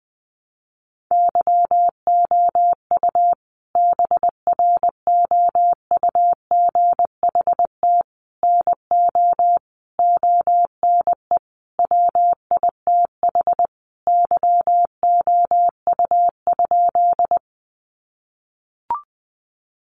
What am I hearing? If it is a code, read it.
Morse code, heard as a keyed tone, about 20 wpm, 700 Hz. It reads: YOU BROUGHT NO ONE WITH YOU?
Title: Sentences from Top 500 Words 20wpm